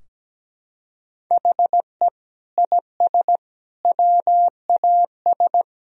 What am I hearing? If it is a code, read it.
Morse code, heard as a keyed tone, about 17 wpm, 700 Hz. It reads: HE IS WAS
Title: Sentences from Top 1000 Words - Review Part 2 - 17wpm